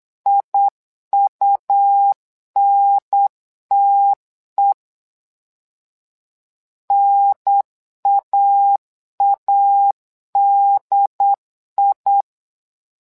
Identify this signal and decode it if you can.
Morse code, heard as a keyed tone, about 8 wpm, 800 Hz. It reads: IUNTE NAADI
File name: LEZIONI / 2 UHDN / 2 UHDN 8wpm2232